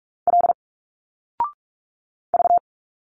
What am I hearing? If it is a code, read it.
Morse code, heard as a keyed tone, about 43 wpm, 700 Hz. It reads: L V
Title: Single Letters 45wpm